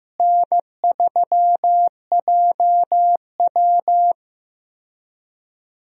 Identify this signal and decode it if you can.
Morse code, heard as a keyed tone, about 15 wpm, 700 Hz. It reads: N3JW
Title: USA Call Signs - 4 Characters - Repeated 3x - 15wpm